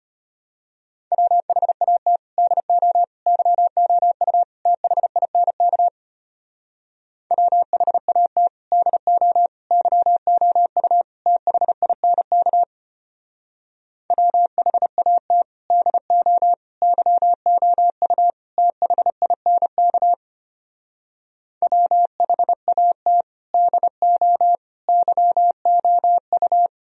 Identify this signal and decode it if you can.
Morse code, heard as a keyed tone, about 33 wpm, 690 Hz. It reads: WHAT DO YOU THINK WHAT DO YOU THINK WHAT DO YOU THINK WHAT DO YOU